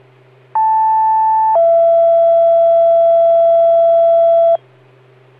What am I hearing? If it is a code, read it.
Two-tone page, heard as a one-second tone, then a three-second tone, about 880 then 670 Hz. It is heard from 0.6 to 4.6 s.